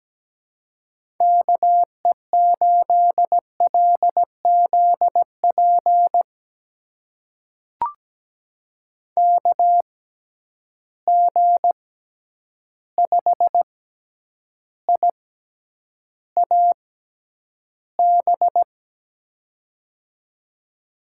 Morse code, heard as a keyed tone, about 17 wpm, spaced as for 9 wpm, 700 Hz. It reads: KE8LZP KG5IAB
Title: USA Call Signs - 6 Characters - 6x Character Spacing - 17wpm